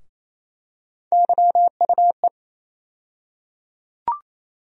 Morse code, heard as a keyed tone, about 28 wpm, 700 Hz. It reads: YUE
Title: USA Call Sign Postfixes - 3 Letters 28wpm